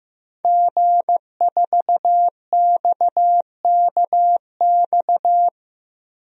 Morse code, heard as a keyed tone, about 15 wpm, 700 Hz. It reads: G4XKX
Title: DXCC-100 Call Signs - 5 Characters - 15wpm